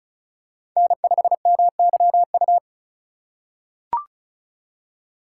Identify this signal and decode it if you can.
Morse code, heard as a keyed tone, about 36 wpm, 700 Hz. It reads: N5MYU
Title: USA Call Signs - 4x Character Spacing - 35wpm